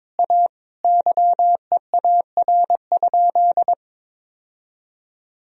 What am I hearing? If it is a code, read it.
Morse code, heard as a keyed tone, about 22 wpm, 700 Hz. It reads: A YEAR?
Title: Sentences from Top 1000 Words - Review Part 2 - 22wpm